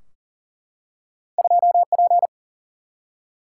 Morse code, heard as a keyed tone, about 38 wpm, 700 Hz. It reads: 2P
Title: Single-Letter Single-Number 40wpm